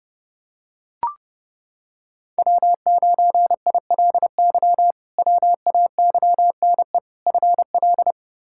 Morse code, heard as a keyed tone, about 32 wpm, 700 Hz. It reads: W9ILY WAYNE FL
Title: K1USN SST QSO Practice - 30wpm